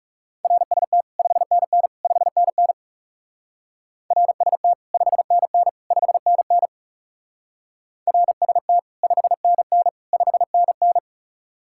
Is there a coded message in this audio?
Yes: Morse code from 0.4 to 11.0 s, heard as a keyed tone, about 38 wpm, 700 Hz. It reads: RST 5NN 5NN RST 5NN 5NN RST 5NN 5NN